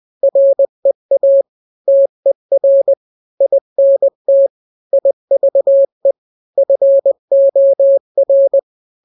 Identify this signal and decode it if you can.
Morse code, heard as a keyed tone, about 20 wpm, 550 Hz. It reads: REA TER INT IVE FOR
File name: n-grams / 3-grams 20 18 WPM